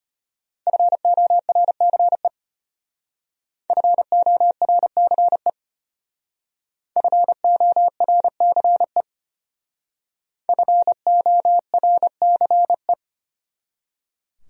Morse code, heard as a keyed tone, about 33 wpm, 700 Hz. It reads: FORCE FORCE FORCE FORCE